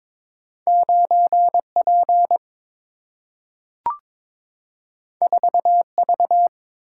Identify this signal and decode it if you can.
Morse code, heard as a keyed tone, about 22 wpm, 700 Hz. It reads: 9P 4V
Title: Single-Letter Single-Number 22wpm